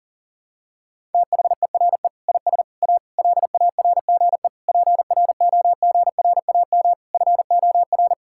Morse code, heard as a keyed tone, about 40 wpm, 700 Hz. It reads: THERE IS A LARGE PROGRAM FOR
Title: Sentences from Top 1000 Words - Review Part 5 - 40wpm